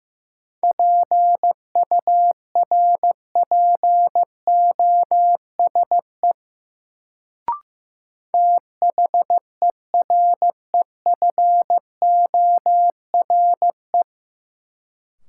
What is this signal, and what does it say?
Morse code, heard as a keyed tone, about 15 wpm, 700 Hz. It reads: PURPOSE THEREFORE
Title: Sentences from Top 800 Words - New Word Drill - 15wpm